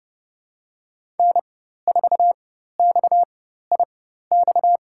Morse code, heard as a keyed tone, about 30 wpm, spaced as for 14 wpm, 700 Hz. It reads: N4XIX
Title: USA Call Signs - 5 Characters - 4x Character Spacing - 30wpm